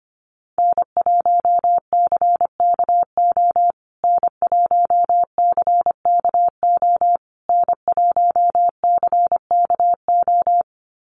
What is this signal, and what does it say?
Morse code, heard as a keyed tone, about 25 wpm, 700 Hz. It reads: N1CKO N1CKO N1CKO
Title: USA Call Signs Repeated 3x 25wpm